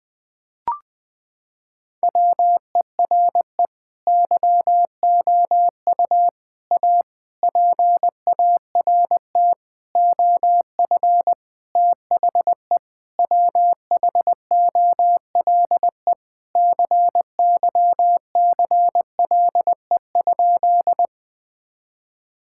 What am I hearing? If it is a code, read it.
Morse code, heard as a keyed tone, about 21 wpm, 700 Hz. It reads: WERE YOU A PART OF THE WHOLE CYCLE?
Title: Sentences from Top 1000 Words - Review Part 6 - 20wpm